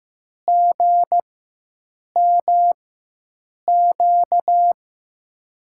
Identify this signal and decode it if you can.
Morse code, heard as a keyed tone, about 16 wpm, spaced as for 7 wpm, 700 Hz. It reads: GMQ
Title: USA Call Signs - 4x Character Spacing - 15wpm